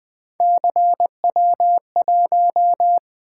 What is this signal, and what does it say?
Morse code, heard as a keyed tone, about 20 wpm, 700 Hz. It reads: CW1
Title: Three Letter-Number Combinations - 20wpm